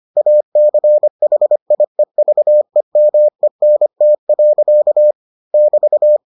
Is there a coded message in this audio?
Yes: Morse code from 0.2 to 6.2 s, heard as a keyed tone, about 25 wpm, 600 Hz. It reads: ACHIEVEMENT. <BT>